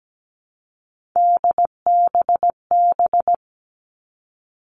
Morse code, heard as a keyed tone, about 17 wpm, 700 Hz. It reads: DBB
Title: All Cut Numbers - 3-Digit 17wpm